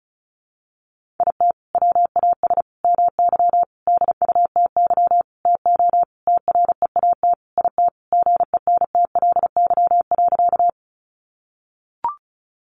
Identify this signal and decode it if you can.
Morse code, heard as a keyed tone, about 34 wpm, 700 Hz. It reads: IT WAS MY DUTY TO TREAT IT GENTLY.
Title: Sentences from Top 1500 Words 35wpm